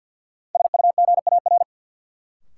Morse code, heard as a keyed tone, about 48 wpm, 700 Hz. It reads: SUGAR
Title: Sentences from Top 1200 Words - New Word Drill Alt - 50wpm